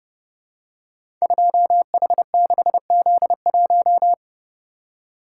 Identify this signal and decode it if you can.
Morse code, heard as a keyed tone, about 29 wpm, 700 Hz. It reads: 2H6Z1
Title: Code Groups - 5 Letter:Numbers - 30wpm